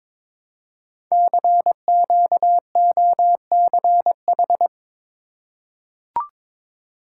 Morse code, heard as a keyed tone, about 23 wpm, 700 Hz. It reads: CQOCH
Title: Code Groups - 5 Letter:Numbers - 22wpm